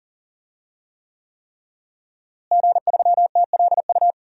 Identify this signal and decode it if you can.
Morse code, heard as a keyed tone, about 40 wpm, 700 Hz. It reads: G3TLU
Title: DXCC-100 Call Signs - 5 Characters - Repeated 3x - 40wpm